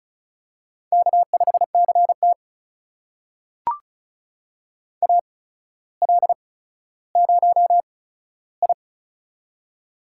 Morse code, heard as a keyed tone, about 36 wpm, spaced as for 15 wpm, 700 Hz. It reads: K5CT AL0I
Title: USA Call Signs - 4 Characters - 8x Character Spacing - 35wpm